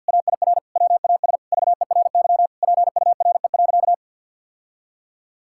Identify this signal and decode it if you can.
Morse code, heard as a keyed tone, about 50 wpm, 700 Hz. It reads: AIR WAS VERY PURE.
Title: Sentences from Top 1600 Words 50wpm